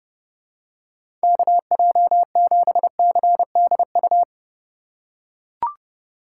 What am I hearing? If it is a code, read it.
Morse code, heard as a keyed tone, about 30 wpm, 700 Hz. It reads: KJ7CDU
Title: USA Call Signs - 4x Character Spacing - 30wpm